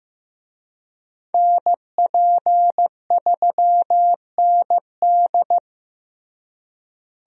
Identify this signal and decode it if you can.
Morse code, heard as a keyed tone, about 16 wpm, 700 Hz. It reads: NP3ND
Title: USA Call Signs - 5 Characters - Repeated 3x - 15wpm